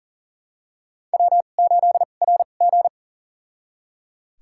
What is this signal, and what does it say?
Morse code, heard as a keyed tone, about 37 wpm, spaced as for 28 wpm, 700 Hz. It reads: W8RG
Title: USA Call Signs - 4 Characters - 2x Character Spacing - 40wpm